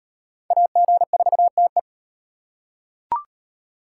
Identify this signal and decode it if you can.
Morse code, heard as a keyed tone, about 38 wpm, 700 Hz. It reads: AG4TE